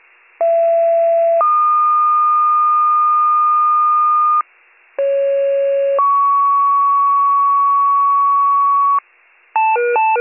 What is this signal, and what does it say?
Two-tone page, heard as a one-second tone, then a three-second tone, about 670 then 1220 Hz, about 550 then 1120 Hz. It is heard from 0.4 to 9.0 s.